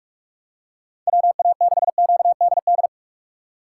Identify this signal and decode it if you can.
Morse code, heard as a keyed tone, about 47 wpm, 700 Hz. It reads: WA6QDD
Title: USA Call Signs - 6 Characters - 6x Character Spacing - 45wpm